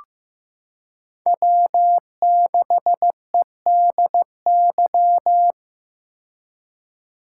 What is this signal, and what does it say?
Morse code, heard as a keyed tone, about 16 wpm, 700 Hz. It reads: W6EDY
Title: K1USN SST QSO Practice - 15wpm